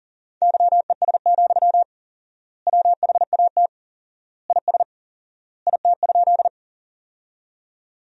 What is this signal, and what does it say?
Morse code, heard as a keyed tone, about 40 wpm, 700 Hz. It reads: YES, WHAT IS IT?
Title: 4x Word-Spacing - Sentences from Top 500 Words - 40wpm